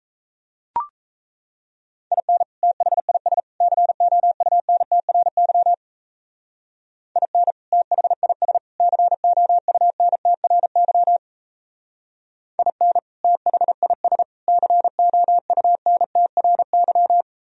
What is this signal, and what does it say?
Morse code, heard as a keyed tone, about 37 wpm, 700 Hz. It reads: IN THIS COUNTRY IN THIS COUNTRY IN THIS COUNTRY